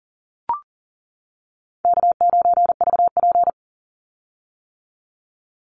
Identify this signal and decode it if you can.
Morse code, heard as a keyed tone, about 40 wpm, 700 Hz. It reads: K9VP
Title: Call Signs - 40wpm